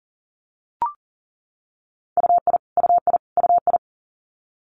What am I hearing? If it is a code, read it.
Morse code, heard as a keyed tone, about 40 wpm, 700 Hz. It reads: UI UI UI
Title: Multi-letter Phonemes Repeated 3x 40wpm